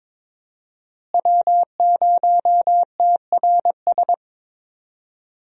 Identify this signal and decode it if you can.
Morse code, heard as a keyed tone, about 21 wpm, 700 Hz. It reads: W0TRS
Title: USA Call Signs - 5 Characters - Repeated 2x - 22wpm